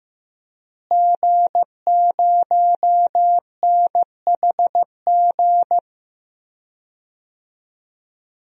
Morse code, heard as a keyed tone, about 15 wpm, 700 Hz. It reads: G0NHG
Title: DXCC-100 Call Signs - 5 Characters - Repeated 2x - 15wpm